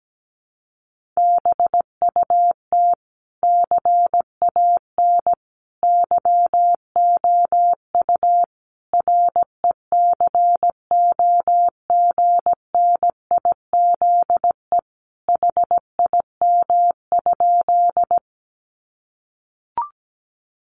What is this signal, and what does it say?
Morse code, heard as a keyed tone, about 17 wpm, 700 Hz. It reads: BUT CAN YOU RECOGNIZE HIM?
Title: Sentences from Top 1400 Words 17wpm